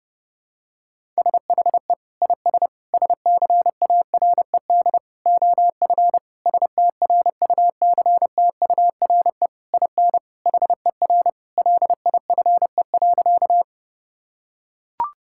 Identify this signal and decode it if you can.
Morse code, heard as a keyed tone, about 32 wpm, 700 Hz. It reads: SHE IS SCARED OF STRUCTURE IN HER LIFE.
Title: Sentences from Top 1000 Words - Review Part 6 - 30wpm